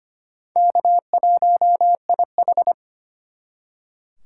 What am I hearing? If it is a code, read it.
Morse code, heard as a keyed tone, about 34 wpm, spaced as for 30 wpm, 700 Hz. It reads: K1IH
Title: Speed Racing - USA 1x2 Call Signs 25wpm